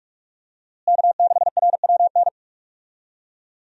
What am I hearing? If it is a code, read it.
Morse code, heard as a keyed tone, about 41 wpm, 700 Hz. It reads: K6RWN
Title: USA Call Signs - 5 Characters - Repeated 2x - 45wpm